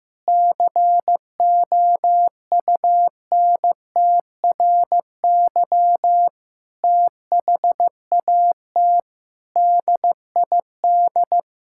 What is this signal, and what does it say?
Morse code, heard as a keyed tone, about 15 wpm, 700 Hz. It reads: COUNTRY THAT DID
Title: Sentences from Top 1600 Words 15wpm